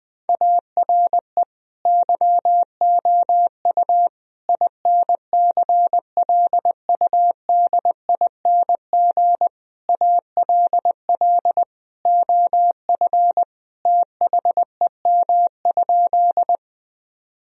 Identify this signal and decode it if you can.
Morse code, heard as a keyed tone, about 20 wpm, 700 Hz. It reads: ARE YOU INCLUDING ALL OF THEM?
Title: Sentences from Top 1000 Words - Review Part 4 - 20wpm